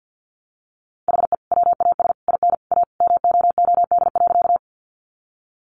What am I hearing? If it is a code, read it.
Morse code, heard as a keyed tone, about 48 wpm, 700 Hz. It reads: HE WAS IN A MOOD.
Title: Sentences from Top 1100 Words 50wpm